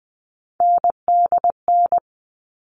Morse code, heard as a keyed tone, about 20 wpm, 700 Hz. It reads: NDN
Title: All Cut Numbers - 3-Digit 20wpm